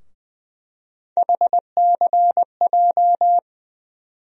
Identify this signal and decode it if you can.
Morse code, heard as a keyed tone, about 20 wpm, 700 Hz. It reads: HCJ
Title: USA Call Sign Postfixes - 3 Letters 20wpm